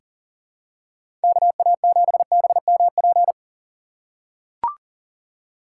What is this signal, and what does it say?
Morse code, heard as a keyed tone, about 38 wpm, 700 Hz. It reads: KA7BMP
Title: USA Call Signs - 2x Character Spacing - 40wpm